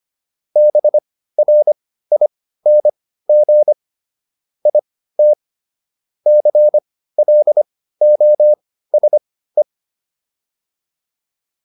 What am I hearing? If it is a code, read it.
Morse code, heard as a keyed tone, about 25 wpm, spaced as for 15 wpm, 600 Hz. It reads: BRING IT CLOSE